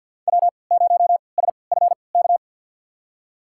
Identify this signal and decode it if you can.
Morse code, heard as a keyed tone, about 53 wpm, spaced as for 29 wpm, 700 Hz. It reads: W0SFK
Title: USA Call Signs - 5 Characters - 3x Character Spacing - 50wpm